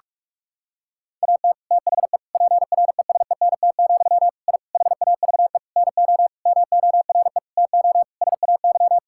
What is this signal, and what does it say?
Morse code, heard as a keyed tone, about 41 wpm, 700 Hz. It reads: AT THE PRESENT, I HAVE NO MORE TO SAY
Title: Sentences from Top 1000 Words - Review Part 5 - 45wpm